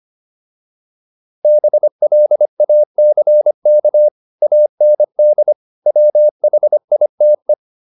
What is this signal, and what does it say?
Morse code, heard as a keyed tone, about 25 wpm, 600 Hz. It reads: BLACK AND WHITE
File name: PR403 25